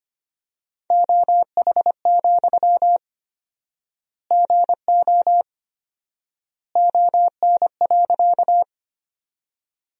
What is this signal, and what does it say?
Morse code, heard as a keyed tone, about 25 wpm, 700 Hz. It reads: OH, GO ON.
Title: 4x Word-Spacing - Sentences from Top 500 Words - 25wpm